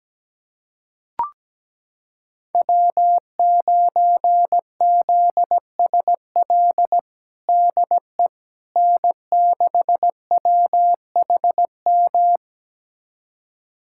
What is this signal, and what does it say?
Morse code, heard as a keyed tone, about 17 wpm, 700 Hz. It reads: W9ZSL DE N6WHM
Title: USA Callsign DE Callsign - 5 Characters - 17wpm